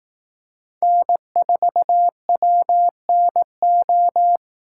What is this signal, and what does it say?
Morse code, heard as a keyed tone, about 18 wpm, 700 Hz. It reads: N4WNO
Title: Speed Racing - USA 1x3 Call Signs 15wpm